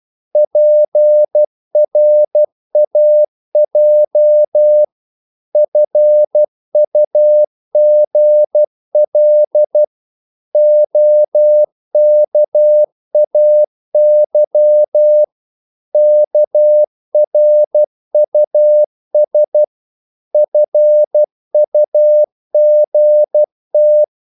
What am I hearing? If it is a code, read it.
Morse code, heard as a keyed tone, about 12 wpm, 600 Hz. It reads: PRAJ FUGL OKAY KRUS FUGT
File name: Korte ord DK 12wpm